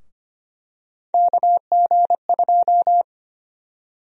Morse code, heard as a keyed tone, about 25 wpm, 700 Hz. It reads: KG2
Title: USA Call Sign Prefixes - Repeated 2x - 25wpm